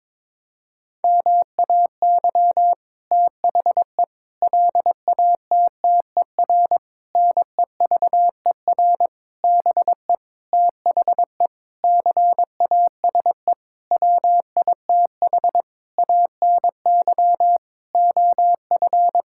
Morse code, heard as a keyed tone, about 22 wpm, 700 Hz. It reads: MAY THE LATTER NEVER BE THE CASE WITH ANY OF
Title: Sentences from Top 1600 Words 22wpm